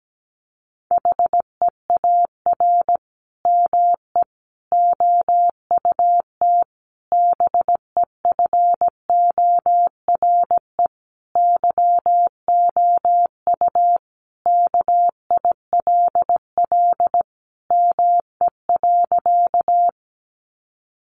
Morse code, heard as a keyed tone, about 17 wpm, 700 Hz. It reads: HEAR ME OUT BEFORE YOU KILL ME.
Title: Sentences from Top 1400 Words 17wpm